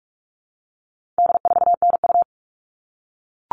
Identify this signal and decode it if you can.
Morse code, heard as a keyed tone, about 44 wpm, 700 Hz. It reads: D4NU
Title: All Cut Numbers - 4-Digit 45wpm